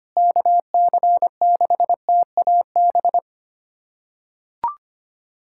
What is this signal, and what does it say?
Morse code, heard as a keyed tone, about 25 wpm, 700 Hz. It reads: KC6TAB
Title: USA Callsign DE Callsign 25wpm